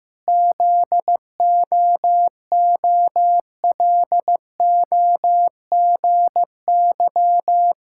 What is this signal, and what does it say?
Morse code, heard as a keyed tone, about 16 wpm, 700 Hz. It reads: ZOOLOGY
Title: Pangram Challenge v2 - 15wpm